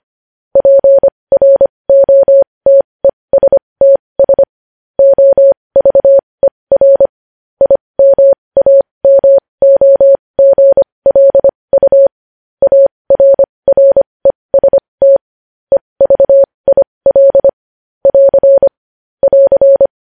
Morse code, heard as a keyed tone, about 25 wpm, spaced as for 20 wpm, 550 Hz. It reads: PROTESTS OVER IMAMOGLU ARREST EVIL <AR> <AR>
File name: podcast / News headlines-1x-25x20